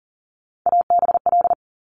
Morse code, heard as a keyed tone, about 42 wpm, 700 Hz. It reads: ABL
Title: Top Three Letter Combinations in Words 40wpm